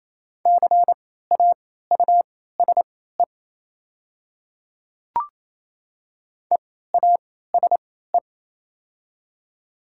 Morse code, heard as a keyed tone, about 28 wpm, spaced as for 16 wpm, 700 Hz. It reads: CAUSE EASE
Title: 3x Character Spacing - Top 300 Words - 28wpm